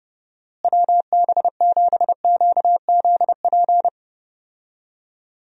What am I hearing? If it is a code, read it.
Morse code, heard as a keyed tone, about 30 wpm, 700 Hz. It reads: WB7QZP
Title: USA Call Signs - 6 Characters - Repeated 2x - 30wpm